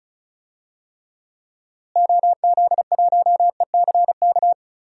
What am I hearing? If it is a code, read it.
Morse code, heard as a keyed tone, about 36 wpm, 700 Hz. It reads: OZ1ECK
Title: DXCC-100 Call Signs - 6 Characters - Repeated 3x - 35wpm